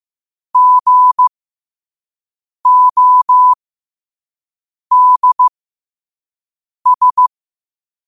Morse code, heard as a keyed tone, about 15 wpm, spaced as for 5 wpm, 1000 Hz. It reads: GODS
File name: qotdfiles / 5wpm